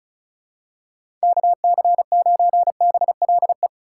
Morse code, heard as a keyed tone, about 34 wpm, 700 Hz. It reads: KC9BLE
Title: USA Call Signs - 4x Character Spacing - 35wpm